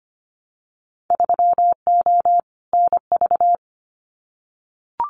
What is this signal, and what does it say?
Morse code, heard as a keyed tone, about 25 wpm, 700 Hz. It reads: 3O NV
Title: Winter Field Day Exchanges Encore 25wpm